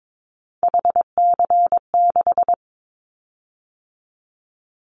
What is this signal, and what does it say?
Morse code, heard as a keyed tone, about 22 wpm, 700 Hz. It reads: HC6
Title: Callsign Prefixes for Top 100 DXCC - 22wpm